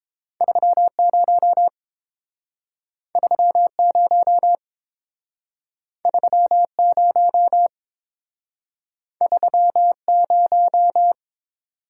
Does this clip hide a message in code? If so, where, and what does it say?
Yes: Morse code from 0.4 to 11.1 s, heard as a keyed tone, about 30 wpm, 700 Hz. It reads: 30 30 30 30